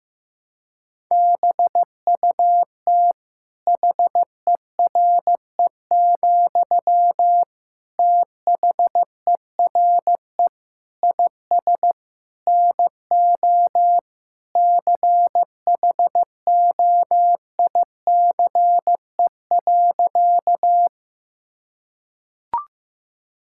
Morse code, heard as a keyed tone, about 15 wpm, 700 Hz. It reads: BUT HERE, THERE IS NO CHOICE.
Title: Sentences from Top 1000 Words - Review Part 6 - 15wpm